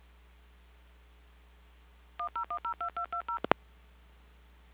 DTMF, heard as telephone touch tones, tones of about 100 ms. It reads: 10102220
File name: recordings / PortlandUnion Millennium